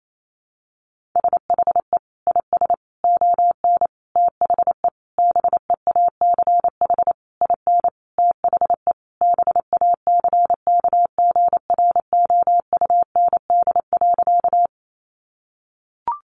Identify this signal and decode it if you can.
Morse code, heard as a keyed tone, about 28 wpm, 700 Hz. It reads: SHE IS ON THE BEACH IN THE BACKGROUND.
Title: Sentences from Top 1500 Words 28wpm